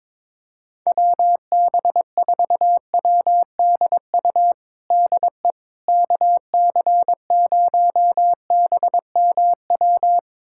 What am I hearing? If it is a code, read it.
Morse code, heard as a keyed tone, about 22 wpm, 700 Hz. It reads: WB4WDU DE KC0BMW
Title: USA Callsign DE Callsign - 6 Characters - 22wpm